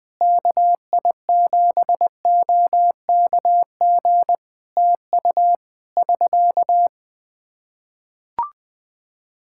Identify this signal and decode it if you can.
Morse code, heard as a keyed tone, about 20 wpm, 700 Hz. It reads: KI7OKG TU <SK>
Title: Practice QSO - Exchange 4 - Ending - 20wpm